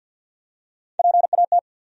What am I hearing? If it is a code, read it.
Morse code, heard as a keyed tone, about 53 wpm, 700 Hz. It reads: PAT
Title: USA Common Names - Max 4 Letters - 50wpm